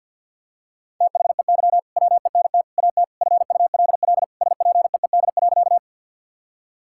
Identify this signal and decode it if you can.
Morse code, heard as a keyed tone, about 50 wpm, 700 Hz. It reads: THEY WENT AT FULL SPEED.
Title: Sentences from Top 1000 Words - Review Part 5 - 50wpm